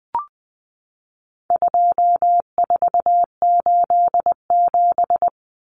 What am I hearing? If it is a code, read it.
Morse code, heard as a keyed tone, about 20 wpm, 700 Hz. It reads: 2487
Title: Numbers - 4-Digit 20wpm